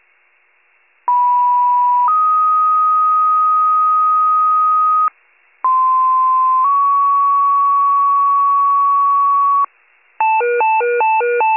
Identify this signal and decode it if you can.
Two-tone page, heard as a one-second tone, then a three-second tone, about 980 then 1250 Hz, about 1040 then 1120 Hz. It is heard from 1.1 to 9.6 s.
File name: pager / 91 FIRE / wcalert